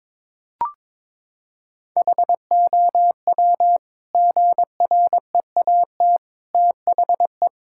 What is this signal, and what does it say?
Morse code, heard as a keyed tone, about 22 wpm, 700 Hz. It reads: HOW GREAT THE
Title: Sentences from Top 1000 Words - Review Part 5 - 22wpm